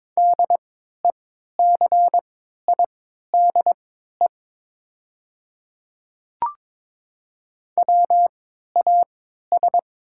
Morse code, heard as a keyed tone, about 22 wpm, spaced as for 12 wpm, 700 Hz. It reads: DECIDE WAS
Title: 3x Character Spacing - Top 500 Words - 22wpm